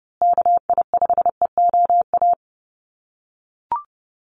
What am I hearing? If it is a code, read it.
Morse code, heard as a keyed tone, about 30 wpm, 700 Hz. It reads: KI5EOA
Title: USA Call Signs Repeated 3x 30wpm